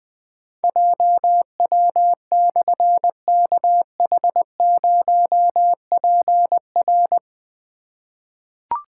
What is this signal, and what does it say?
Morse code, heard as a keyed tone, about 20 wpm, 700 Hz. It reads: JW/KH0PR
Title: Difficult Call Signs 20wpm